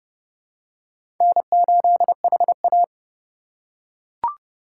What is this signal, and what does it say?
Morse code, heard as a keyed tone, about 28 wpm, 700 Hz. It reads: N8HA